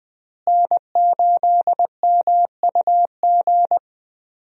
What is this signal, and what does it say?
Morse code, heard as a keyed tone, about 20 wpm, 700 Hz. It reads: N8MUG
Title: USA Call Signs - 5 Characters - Repeated 3x - 20wpm